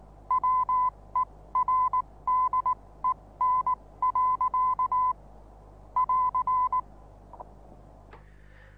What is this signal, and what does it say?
Morse code, heard as a keyed tone, about 19 wpm, spaced as for 16 wpm, 1010 Hz. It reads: WERDEN. <AR>
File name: repos / Morsekurs / DJ4UF Morsen26